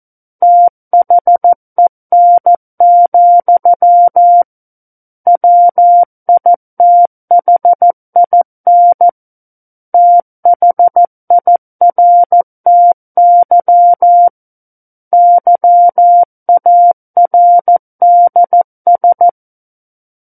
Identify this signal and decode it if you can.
Morse code, heard as a keyed tone, about 14 wpm, 700 Hz. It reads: THEN, WITHIN THIRTY YARDS
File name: Book One / 05-Chapter 5 - 14 WPM